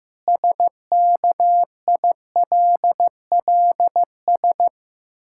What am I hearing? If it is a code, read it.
Morse code, heard as a keyed tone, about 15 wpm, 700 Hz. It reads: SKILLS
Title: Pangram Challenge v2 - 15wpm